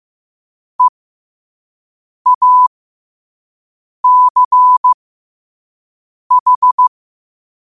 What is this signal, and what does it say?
Morse code, heard as a keyed tone, about 15 wpm, spaced as for 5 wpm, 1000 Hz. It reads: EACH
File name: qotdfiles / 5wpm